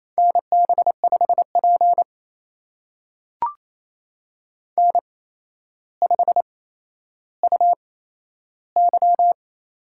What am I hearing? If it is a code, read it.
Morse code, heard as a keyed tone, about 28 wpm, spaced as for 12 wpm, 700 Hz. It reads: NB5P N5UY
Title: USA Call Signs - 4 Characters - 8x Character Spacing - 28wpm